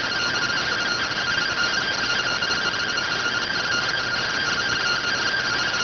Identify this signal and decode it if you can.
Two-tone teleprinter signal, about 170 Hz shift.